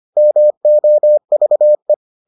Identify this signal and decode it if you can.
Morse code, heard as a keyed tone, about 25 wpm, 600 Hz. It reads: MOVE